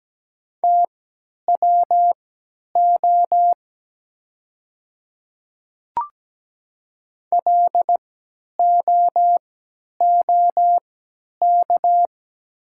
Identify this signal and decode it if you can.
Morse code, heard as a keyed tone, about 17 wpm, spaced as for 10 wpm, 700 Hz. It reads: TWO LOOK
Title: 3x Character Spacing - Top 200 Words - 17wpm